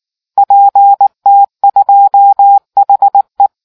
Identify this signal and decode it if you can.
Morse code, heard as a keyed tone, about 19 wpm, 800 Hz. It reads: PT2HE